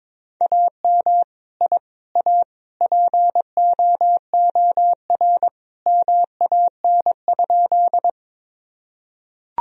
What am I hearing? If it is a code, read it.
Morse code, heard as a keyed tone, about 22 wpm, 700 Hz. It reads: AM I A POOR MAN?
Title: Sentences from Top 1000 Words - Review Part 2 - 22wpm